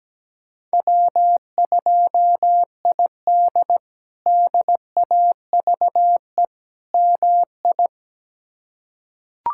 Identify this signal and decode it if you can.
Morse code, heard as a keyed tone, about 17 wpm, 700 Hz. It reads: W2ID DAVE MI
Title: K1USN SST QSO Practice - 17wpm